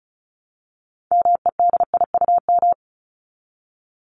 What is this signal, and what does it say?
Morse code, heard as a keyed tone, about 34 wpm, 700 Hz. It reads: MEDIUM
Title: Top 751-1000 Words 35wpm